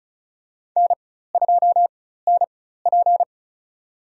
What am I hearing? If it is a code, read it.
Morse code, heard as a keyed tone, about 36 wpm, spaced as for 16 wpm, 700 Hz. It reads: N2NP
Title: USA Call Signs - 4x Character Spacing - 35wpm